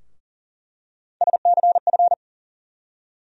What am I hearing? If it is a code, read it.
Morse code, heard as a keyed tone, about 40 wpm, 700 Hz. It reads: SCF
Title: USA Call Sign Postfixes - 3 Letters 40wpm